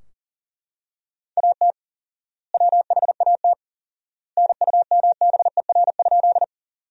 Morse code, heard as a keyed tone, about 40 wpm, 700 Hz. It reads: AT WHAT NUMBER?
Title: 4x Word-Spacing - Sentences from Top 100 Words - 40wpm